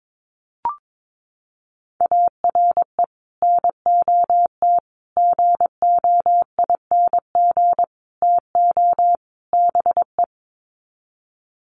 Morse code, heard as a keyed tone, about 22 wpm, 700 Hz. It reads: ARE NOT GOING TO BE
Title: Sets of 5 Words Encore 22wpm